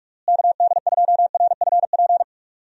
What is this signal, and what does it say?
Morse code, heard as a keyed tone, about 46 wpm, 700 Hz. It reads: KD2RFP
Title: USA Call Signs - 6 Characters - Repeated 3x - 45wpm